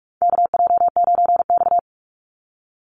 Morse code, heard as a keyed tone, about 44 wpm, 700 Hz. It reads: KJ9X
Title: NA SOTA Chaser Callsigns 45wpm